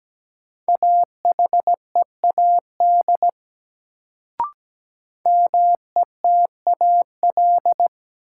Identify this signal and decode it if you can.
Morse code, heard as a keyed tone, about 17 wpm, 700 Hz. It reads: AHEAD METAL
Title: Sentences from Top 1000 Words - Review Part 6 - New Word Drill - 17wpm